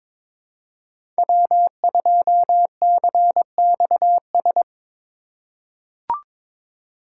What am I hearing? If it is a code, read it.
Morse code, heard as a keyed tone, about 22 wpm, 700 Hz. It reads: W2CXS
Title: Code Groups - 5 Letter:Numbers - 22wpm